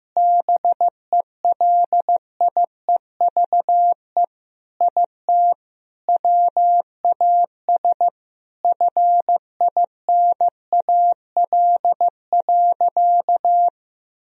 Morse code, heard as a keyed tone, about 15 wpm, 700 Hz. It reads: BELIEVE IT WAS FINAL.
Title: Sentences from Top 1000 Words - Review Part 2 - 15wpm